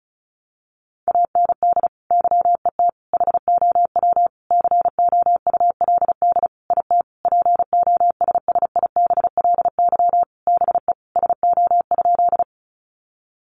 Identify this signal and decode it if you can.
Morse code, heard as a keyed tone, about 34 wpm, 700 Hz. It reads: AND YET HOW COULD IT POSSIBLY BE SO?
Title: Sentences from Top 1300 Words 35wpm